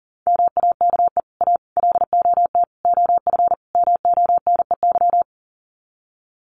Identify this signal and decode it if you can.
Morse code, heard as a keyed tone, about 37 wpm, 700 Hz. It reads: MAKE A LOT OF MONEY
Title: Sets of 5 Words Encore2 40wpm